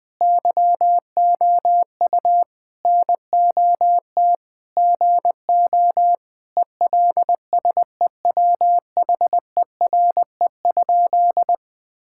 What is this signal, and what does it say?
Morse code, heard as a keyed tone, about 20 wpm, 700 Hz. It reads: YOU NOT GO ELSEWHERE?
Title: Sentences from Top 1600 Words 20wpm